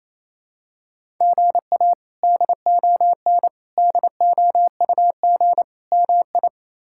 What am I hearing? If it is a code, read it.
Morse code, heard as a keyed tone, about 27 wpm, 700 Hz. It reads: GA DON DOUG MI
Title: K1USN SST QSO Practice - 28wpm